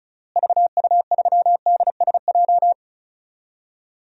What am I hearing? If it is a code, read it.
Morse code, heard as a keyed tone, about 36 wpm, 700 Hz. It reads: VU3DSJ
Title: DXCC-100 Call Signs - 6 Characters - Repeated 3x - 35wpm